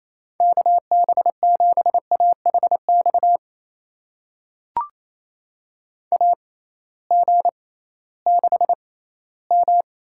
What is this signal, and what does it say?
Morse code, heard as a keyed tone, about 28 wpm, spaced as for 16 wpm, 700 Hz. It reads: KB7AHX AG6M
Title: USA Call Signs - 6x Character Spacing - 28wpm